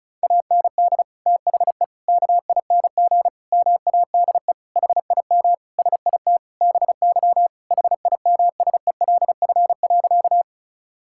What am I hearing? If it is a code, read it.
Morse code, heard as a keyed tone, about 34 wpm, 700 Hz. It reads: AND THE KING MADE HIM SIT BY HIMSELF.
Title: Sentences from Top 1600 Words 35wpm